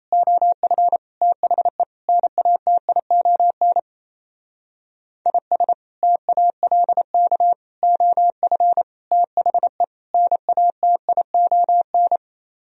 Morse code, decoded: OF THE NATION IS TALK OF THE NATION

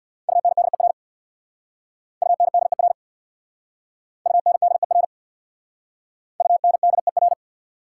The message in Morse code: UNDER UNDER UNDER UNDER

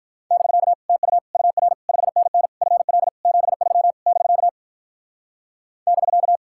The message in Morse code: <BK> TU UR 5NN FL 73 <BK> <BK>

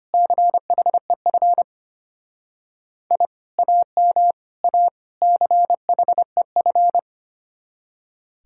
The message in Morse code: CHEF I AM A CHEF

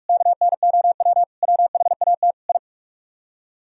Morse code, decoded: KNOW WHAT I

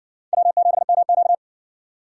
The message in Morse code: W7MQ

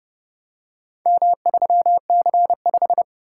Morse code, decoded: M3C5

